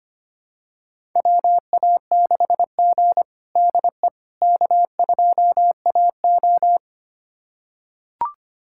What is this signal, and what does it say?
Morse code, heard as a keyed tone, about 25 wpm, 700 Hz. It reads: WA6G DE K2AO
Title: USA Callsign DE Callsign - 4 Characters - 25wpm